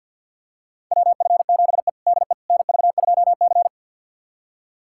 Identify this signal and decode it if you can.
Morse code, heard as a keyed tone, about 47 wpm, 700 Hz. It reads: WF7E DE NV2C
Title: USA Callsign DE Callsign - 4 Characters - 50wpm